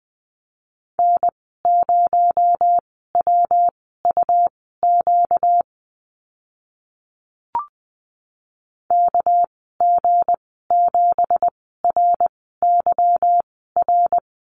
Morse code, heard as a keyed tone, about 20 wpm, spaced as for 14 wpm, 700 Hz. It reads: N0WUQ KG7RYR
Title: 245 Lesson 39g - Letter z - Call Signs 20wpm